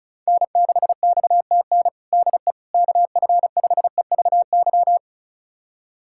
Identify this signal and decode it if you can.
Morse code, heard as a keyed tone, about 36 wpm, 690 Hz. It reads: N6XTN DE KF5EVY
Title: USA Callsign DE Callsign 35wpm